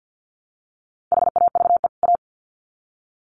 Morse code, heard as a keyed tone, about 50 wpm, 700 Hz. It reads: HAVE A
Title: Sets of 2 Words 50wpm